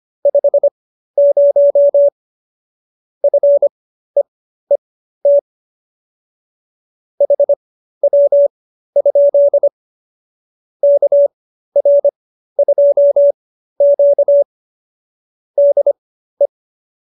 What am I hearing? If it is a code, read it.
Morse code, heard as a keyed tone, about 25 wpm, spaced as for 13 wpm, 570 Hz. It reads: 50 FEET HW? KR2Q DE